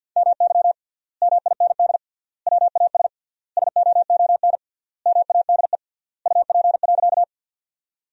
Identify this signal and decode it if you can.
Morse code, heard as a keyed tone, about 50 wpm, 700 Hz. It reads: MY MIND WAS SOON MADE UP.
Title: 3x Word-Spacing - Sentences from Top 500 Words - 50wpm